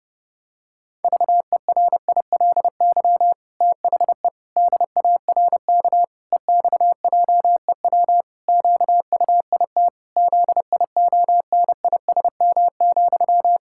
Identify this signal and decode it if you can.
Morse code, heard as a keyed tone, about 32 wpm, 700 Hz. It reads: VERILY THE DARK EXJEW QUIT ZIONISM,
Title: Pangram Challenge v2 - 30wpm